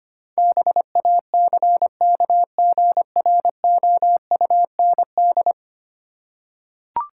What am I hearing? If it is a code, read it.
Morse code, heard as a keyed tone, about 25 wpm, 700 Hz. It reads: BACKGROUND
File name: Ten Letter Words - 25wpm